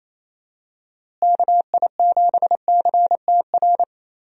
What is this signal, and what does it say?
Morse code, heard as a keyed tone, about 28 wpm, 700 Hz. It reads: KI7CTR
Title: USA Call Signs - 6 Characters - Repeated 2x - 28wpm